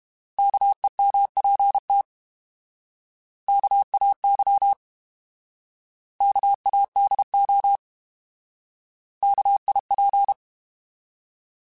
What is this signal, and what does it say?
Morse code, heard as a keyed tone, about 32 wpm, 800 Hz. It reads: KEMPT KAY KADO KIP
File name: Alphabetical / Alphabetical mp3s / Ks @30